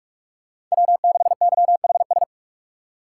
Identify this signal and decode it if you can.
Morse code, heard as a keyed tone, about 46 wpm, 700 Hz. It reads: W6YHS
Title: USA Call Signs - 5 Characters - 2x Character Spacing - 45wpm